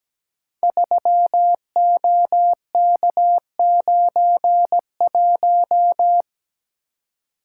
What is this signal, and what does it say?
Morse code, heard as a keyed tone, about 17 wpm, 700 Hz. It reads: 3OK91
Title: Code Groups - 5 Letter:Numbers - 17wpm